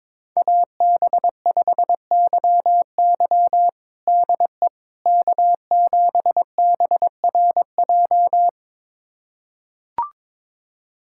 Morse code, heard as a keyed tone, about 22 wpm, 700 Hz. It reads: AB5YY DE K7BRJ
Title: USA Callsign DE Callsign - 5 Characters - 22wpm